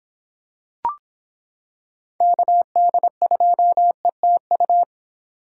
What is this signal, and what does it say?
Morse code, heard as a keyed tone, about 26 wpm, 700 Hz. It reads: KD2ETU